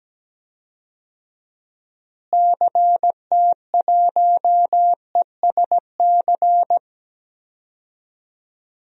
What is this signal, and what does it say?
Morse code, heard as a keyed tone, about 17 wpm, 700 Hz. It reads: CT1ESC
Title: DXCC-100 Call Signs - 6 Characters - Repeated 2x - 17wpm